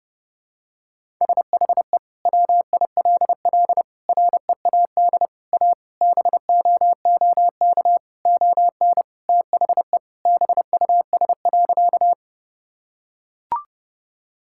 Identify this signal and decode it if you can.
Morse code, heard as a keyed tone, about 32 wpm, 700 Hz. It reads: SHE WILL READ A BOOK ON THE BUS.
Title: Sentences from Top 1000 Words - Review Part 6 - 30wpm